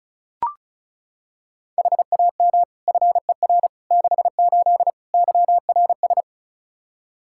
Morse code, heard as a keyed tone, about 36 wpm, 700 Hz. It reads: HAM FER 68 YRS
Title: K1USN SST QSO Practice 35wpm